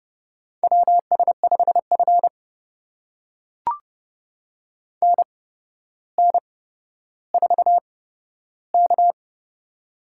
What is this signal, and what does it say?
Morse code, heard as a keyed tone, about 30 wpm, spaced as for 13 wpm, 700 Hz. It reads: WS5F NN4K
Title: USA Call Signs - 4 Characters - 8x Character Spacing - 30wpm